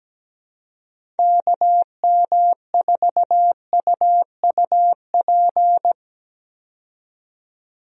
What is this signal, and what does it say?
Morse code, heard as a keyed tone, about 17 wpm, 700 Hz. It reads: KM4UUP
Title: USA Call Signs - 6 Characters - 17wpm